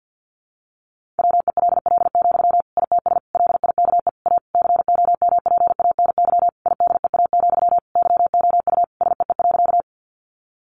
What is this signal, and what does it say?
Morse code, heard as a keyed tone, about 50 wpm, 700 Hz. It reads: WELL, ITS LIKE A COMPANY IDEA, YOU SEE.
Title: Sentences from Top 600 Words 50wpm